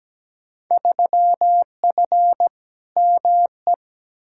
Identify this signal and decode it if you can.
Morse code, heard as a keyed tone, about 17 wpm, 700 Hz. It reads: 3F ME
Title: Field Day Exchanges - 17wpm